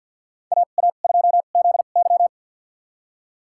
Morse code, heard as a keyed tone, about 52 wpm, spaced as for 36 wpm, 700 Hz. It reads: AA2ZY
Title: USA Call Signs - 5 Characters - 2x Character Spacing - 50wpm